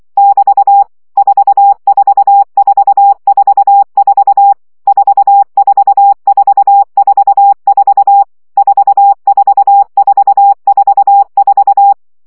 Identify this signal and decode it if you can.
Morse code, heard as a keyed tone, about 24 wpm, 800 Hz. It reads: <BT> 44444 44444 44444
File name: CW gyakorlas 4 20 baud